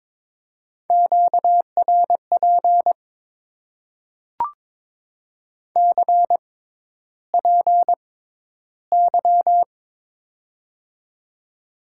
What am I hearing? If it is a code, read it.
Morse code, heard as a keyed tone, about 22 wpm, spaced as for 11 wpm, 700 Hz. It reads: QRP CPY
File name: Top 90 QSO Elements - 6x Character Spacing - 22wpm